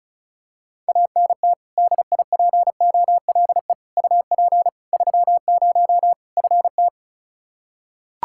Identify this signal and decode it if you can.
Morse code, heard as a keyed tone, about 34 wpm, 700 Hz. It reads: ANT DIPOLE UP 30 FT
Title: K1USN SST QSO Practice 35wpm